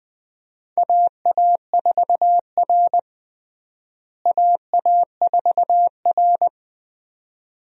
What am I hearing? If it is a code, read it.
Morse code, heard as a keyed tone, about 20 wpm, 700 Hz. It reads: AA4R AA4R
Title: USA Call Signs - 4 Characters - Repeated 2x - 20wpm